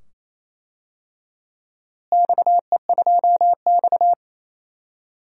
Morse code, heard as a keyed tone, about 28 wpm, 700 Hz. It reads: XE2X
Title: DXCC-100 Call Signs - 4 Characters - Repeated 2x - 28wpm